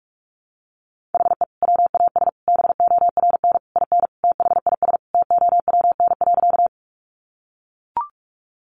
Morse code, heard as a keyed tone, about 44 wpm, 700 Hz. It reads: HE WAS BORN IN THIS TOWN.
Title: Sentences from Top 1000 Words 45wpm